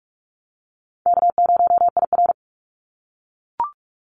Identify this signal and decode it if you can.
Morse code, heard as a keyed tone, about 44 wpm, 700 Hz. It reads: K0IR
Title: Easy Call Signs - 45wpm